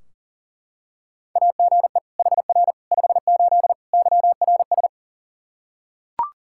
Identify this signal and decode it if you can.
Morse code, heard as a keyed tone, about 40 wpm, 700 Hz. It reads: AGE HR 58 YRS
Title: Practice QSO - Exchange 3 - More Info - 40wpm